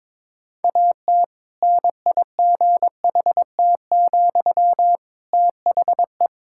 Morse code, heard as a keyed tone, about 21 wpm, 700 Hz. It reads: AT NIGHT, THE
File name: Difficult Sentences with Hints - 22wpm